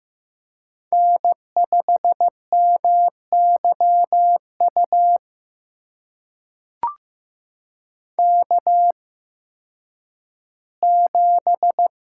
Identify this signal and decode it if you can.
Morse code, heard as a keyed tone, about 15 wpm, spaced as for 10 wpm, 700 Hz. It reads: N5MYU K7
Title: USA Call Signs - 8x Character Spacing - 15wpm